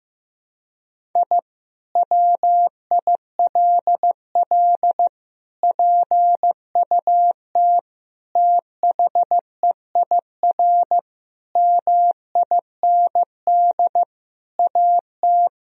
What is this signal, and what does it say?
Morse code, heard as a keyed tone, about 16 wpm, 700 Hz. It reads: I WILL PUT THEIR MIND AT
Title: Sentences from Top 1000 Words - Review Part 1 - 15wpm